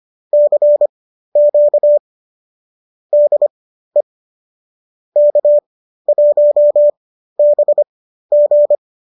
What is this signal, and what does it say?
Morse code, heard as a keyed tone, about 25 wpm, spaced as for 13 wpm, 590 Hz. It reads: CQ DE K1BG